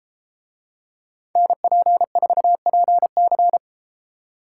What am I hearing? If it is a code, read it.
Morse code, heard as a keyed tone, about 32 wpm, 700 Hz. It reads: NP4PC